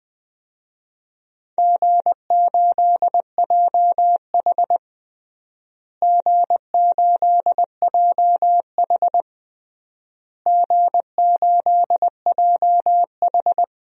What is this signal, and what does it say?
Morse code, heard as a keyed tone, about 21 wpm, 700 Hz. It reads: G8JH G8JH G8JH
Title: DXCC-100 Call Signs - 4 Characters - Repeated 3x - 20wpm